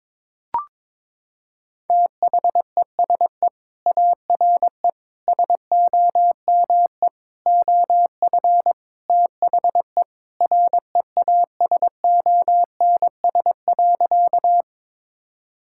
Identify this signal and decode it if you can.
Morse code, heard as a keyed tone, about 21 wpm, 700 Hz. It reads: THESE ARE SOME OF THE REASONS.
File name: practice / 22wpm / Sentences with 4-Word Phrases - 22wpm